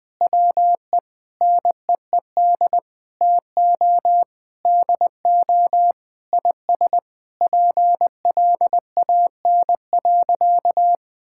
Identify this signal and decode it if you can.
Morse code, heard as a keyed tone, about 21 wpm, 700 Hz. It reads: WE NEED TO DO IS PLAN.
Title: Sentences with 8-Word Phrases - 20wpm